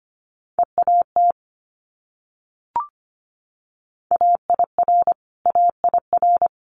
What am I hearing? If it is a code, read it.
Morse code, heard as a keyed tone, about 25 wpm, 700 Hz. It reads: EAT AIR AIR
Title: Multi-letter Phonemes Repeated 3x 25wpm